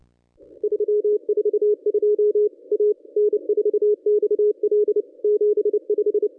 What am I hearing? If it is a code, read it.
Morse code, heard as a keyed tone, about 29 wpm, 420 Hz. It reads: 342 A N4XL 75
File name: contests / ss-cw-k5tr-op-n5rz -1000